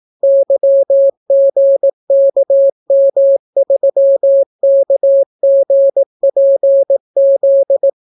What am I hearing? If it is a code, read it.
Morse code, heard as a keyed tone, about 18 wpm, 550 Hz. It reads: YGKM3KGPZ